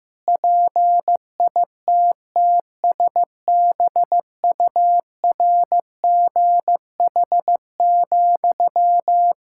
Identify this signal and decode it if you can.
Morse code, heard as a keyed tone, about 15 wpm, 700 Hz. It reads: PITTSBURGH,
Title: Practice QSO - Exchange 1 - Beginning - 15wpm